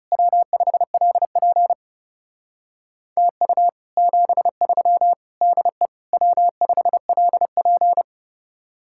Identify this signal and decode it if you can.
Morse code, heard as a keyed tone, about 35 wpm, 700 Hz. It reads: W5LP TU 73 DE W5LP
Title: Speed-Racing Common QSO Phrases - 30wpm